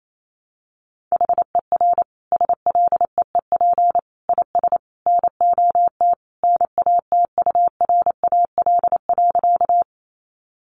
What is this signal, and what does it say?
Morse code, heard as a keyed tone, about 28 wpm, 700 Hz. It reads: HER SLEEP IS NOT NATURAL.
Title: Sentences from Top 700 Words 28wpm